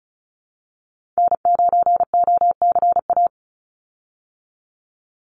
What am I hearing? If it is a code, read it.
Morse code, heard as a keyed tone, about 34 wpm, 700 Hz. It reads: N9OCA
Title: USA Call Signs 35wpm